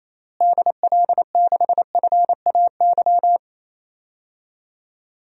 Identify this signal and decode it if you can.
Morse code, heard as a keyed tone, about 28 wpm, 700 Hz. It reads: DL6FAY
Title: DXCC-100 Call Signs - 6 Characters - Repeated 3x - 28wpm